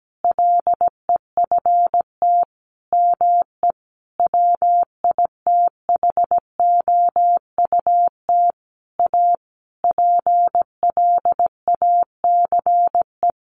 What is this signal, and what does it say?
Morse code, heard as a keyed tone, about 17 wpm, 700 Hz. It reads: LEFT ME WITHOUT A PLACE
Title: Sentences from Top 600 Words 17wpm